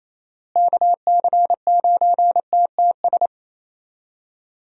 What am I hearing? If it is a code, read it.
Morse code, heard as a keyed tone, about 28 wpm, 700 Hz. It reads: KC9TTS
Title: USA Call Signs - 6 Characters - Repeated 3x - 28wpm